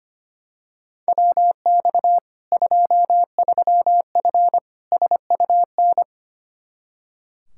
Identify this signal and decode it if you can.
Morse code, heard as a keyed tone, about 25 wpm, 700 Hz. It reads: WX 23F SUN
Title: Practice QSO - Exchange 2 - Info - 25wpm